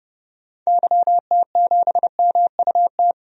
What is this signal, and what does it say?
Morse code, heard as a keyed tone, about 30 wpm, 700 Hz. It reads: YT7MUT
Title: DXCC-100 Call Signs - 6 Characters - Repeated 3x - 30wpm